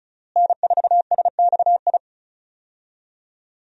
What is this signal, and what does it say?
Morse code, heard as a keyed tone, about 36 wpm, 700 Hz. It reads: N4SXI
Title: USA Call Signs - 5 Characters - Repeated 3x - 35wpm